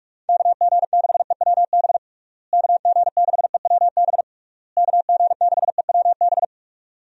Morse code, heard as a keyed tone, about 42 wpm, 700 Hz. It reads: KG6EWB KG6EWB KG6EWB
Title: USA Call Signs - 6 Characters - Repeated 3x - 45wpm